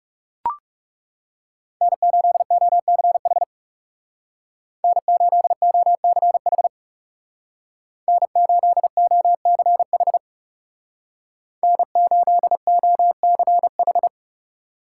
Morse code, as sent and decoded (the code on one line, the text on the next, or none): -. ---.. --- -.-. .... / -. ---.. --- -.-. .... / -. ---.. --- -.-. .... / -. ---.. --- -.-. ....
N8OCH N8OCH N8OCH N8OCH